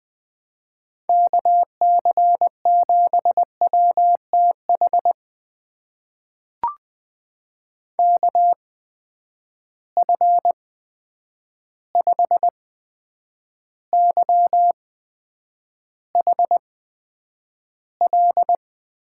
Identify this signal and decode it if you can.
Morse code, heard as a keyed tone, about 20 wpm, spaced as for 9 wpm, 700 Hz. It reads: KC7WTH KF5YHL
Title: USA Call Signs - 6 Characters - 8x Character Spacing - 20wpm